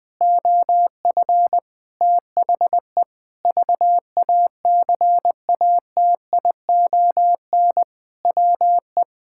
Morse code, heard as a keyed tone, about 20 wpm, 700 Hz. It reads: OF THE VACATION WE
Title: Sentences with 6-Word Phrases - Spoken Beginning - 20wpm